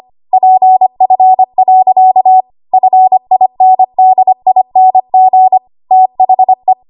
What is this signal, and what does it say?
Morse code, heard as a keyed tone, about 25 wpm, 750 Hz. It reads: PF. FINDING THE
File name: Archive / 25 WPM / 25WPM